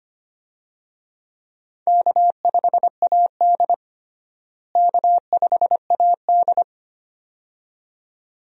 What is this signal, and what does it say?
Morse code, heard as a keyed tone, about 25 wpm, 700 Hz. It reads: K5AD K5AD
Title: USA Call Signs - 4 Characters - Repeated 2x - 25wpm